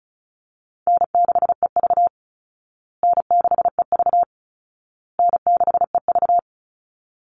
Morse code, heard as a keyed tone, about 34 wpm, 700 Hz. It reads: N6EV N6EV N6EV
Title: Call Signs Repeated 2x - 35wpm